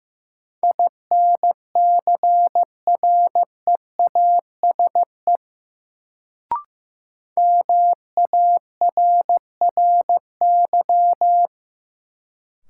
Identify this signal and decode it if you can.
Morse code, heard as a keyed tone, about 15 wpm, 700 Hz. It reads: INCREASE MARRY
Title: Sentences from Top 1200 Words - New Word Drill Alt - 15wpm